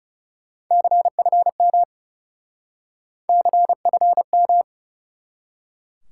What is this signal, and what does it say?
Morse code, heard as a keyed tone, about 40 wpm, 700 Hz. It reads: CFM CFM